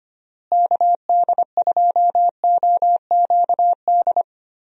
Morse code, heard as a keyed tone, about 25 wpm, 700 Hz. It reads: KD2OQD
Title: USA Call Signs - 6 Characters - Repeated 3x - 25wpm